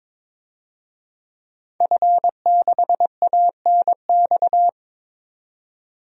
Morse code, heard as a keyed tone, about 22 wpm, 700 Hz. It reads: F6ANX
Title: DXCC-100 Call Signs - 5 Characters - 22wpm